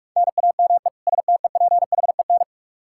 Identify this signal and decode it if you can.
Morse code, heard as a keyed tone, about 46 wpm, 700 Hz. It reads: NAME STEPHEN
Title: Practice QSO - Exchange 1 - Beginning - 45wpm